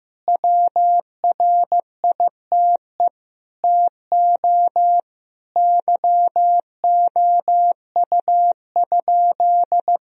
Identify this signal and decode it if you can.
Morse code, heard as a keyed tone, about 15 wpm, 700 Hz. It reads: WRITE TO YOU?
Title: Sentences from Top 1000 Words - Review Part 1 - 15wpm